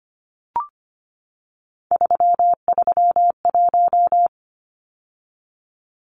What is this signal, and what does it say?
Morse code, heard as a keyed tone, about 25 wpm, 700 Hz. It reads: 331
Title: Numbers - 3-Digit 25wpm